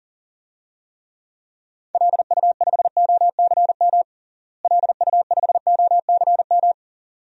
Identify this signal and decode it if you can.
Morse code, heard as a keyed tone, about 40 wpm, 700 Hz. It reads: LU5OCM LU5OCM
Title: DXCC-100 Call Signs - 6 Characters - Repeated 2x - 40wpm